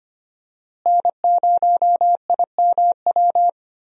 Morse code, heard as a keyed tone, about 25 wpm, 700 Hz. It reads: N0IMW
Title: USA Call Signs - 5 Characters - Repeated 2x - 25wpm